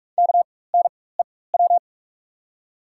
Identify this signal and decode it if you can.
Morse code, heard as a keyed tone, about 48 wpm, spaced as for 21 wpm, 690 Hz. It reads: KNEW
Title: 4x Character Spacing - Top 500 Words - 45wpm